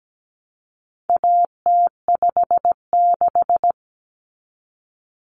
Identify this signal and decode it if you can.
Morse code, heard as a keyed tone, about 17 wpm, 700 Hz. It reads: AT56
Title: Common Cut Numbers - 4-Digit 17wpm